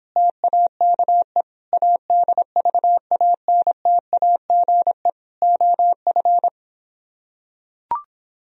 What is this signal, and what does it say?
Morse code, heard as a keyed tone, about 26 wpm, 700 Hz. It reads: TAKE ADVANTAGE OF